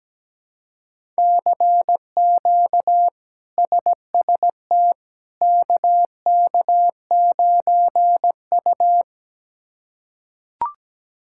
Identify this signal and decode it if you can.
Morse code, heard as a keyed tone, about 17 wpm, 700 Hz. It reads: CQ SST KK9U
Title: K1USN SST QSO Practice - 17wpm